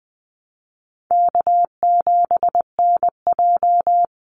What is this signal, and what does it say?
Morse code, heard as a keyed tone, about 20 wpm, 700 Hz. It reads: K7NJ
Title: Easy Call Signs - 20wpm